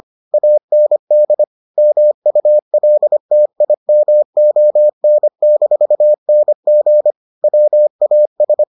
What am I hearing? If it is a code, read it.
Morse code, heard as a keyed tone, about 25 wpm, 600 Hz. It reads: AND MULTIMON-NG WAS